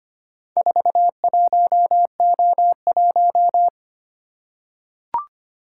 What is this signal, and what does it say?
Morse code, heard as a keyed tone, about 25 wpm, 700 Hz. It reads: 41O1
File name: practice / 25wpm / Four Letter-Number Combinations - 25wpm